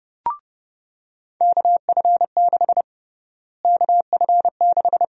Morse code, heard as a keyed tone, about 30 wpm, 700 Hz. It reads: KF6 KF6
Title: USA Call Sign Prefixes - Repeated 2x - 30wpm